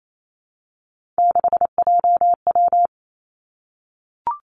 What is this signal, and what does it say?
Morse code, heard as a keyed tone, about 28 wpm, 700 Hz. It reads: 6JW
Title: Three Challenging Letter-Number Combinations 28wpm